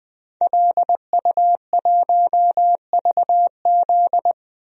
Morse code, heard as a keyed tone, about 20 wpm, 700 Hz. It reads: LU1VZ
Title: DXCC-100 Call Signs - 5 Characters - 20wpm